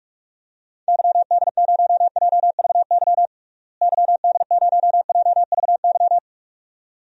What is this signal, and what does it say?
Morse code, heard as a keyed tone, about 44 wpm, 700 Hz. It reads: YD0JVY YD0JVY
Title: DXCC-100 Call Signs - 6 Characters - Repeated 2x - 45wpm